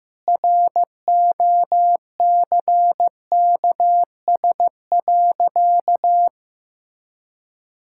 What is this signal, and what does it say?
Morse code, heard as a keyed tone, about 15 wpm, 700 Hz. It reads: ROCKS.
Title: Pangram Challenge v2 - 15wpm